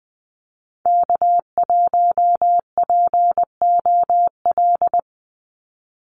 Morse code, heard as a keyed tone, about 20 wpm, 700 Hz. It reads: K1POL
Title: USA Call Signs 20wpm